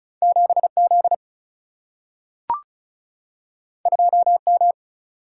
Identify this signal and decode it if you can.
Morse code, heard as a keyed tone, about 36 wpm, 700 Hz. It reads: 7 Z 2 M
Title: Single-Letter Single-Number 35wpm